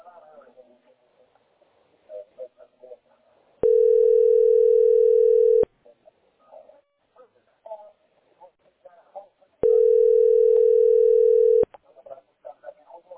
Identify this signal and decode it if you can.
Ringback tone (North America), 440 + 480 Hz, with speech.
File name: 32dc8e6e-70ed-4ea3-9327